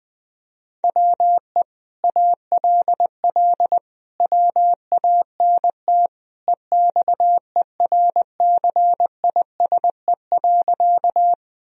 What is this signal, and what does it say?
Morse code, heard as a keyed tone, about 19 wpm, 700 Hz. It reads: WE ALL WANT EXERCISE.
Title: Sentences from Top 1000 Words - Review Part 4 - 20wpm